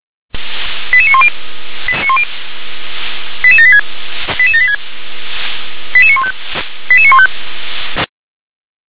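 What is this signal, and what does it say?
Five-tone selective call (ZVEI), heard as quick runs of five tones, tones of about 80 ms. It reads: 88910 88786 88786 88915 88915
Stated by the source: Five-tone ZVEI selective call received with an SDR.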